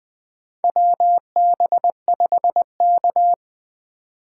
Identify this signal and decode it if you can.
Morse code, heard as a keyed tone, about 21 wpm, 700 Hz. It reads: WB5K
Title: K1USN SST QSO Practice - 20wpm